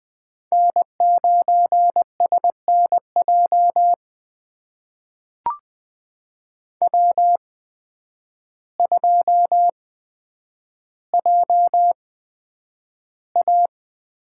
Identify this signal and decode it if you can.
Morse code, heard as a keyed tone, about 20 wpm, spaced as for 9 wpm, 700 Hz. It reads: N9SNJ W2JA
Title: USA Call Signs - 8x Character Spacing - 20wpm